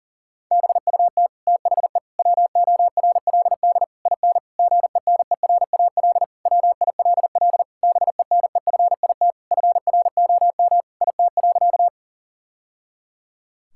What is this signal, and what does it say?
Morse code, heard as a keyed tone, about 40 wpm, 700 Hz. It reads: BUT THE WORLD IN GENERAL WILL BENEFIT FROM IT.
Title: Sentences from Top 1000 Words - Review Part 6 - 40wpm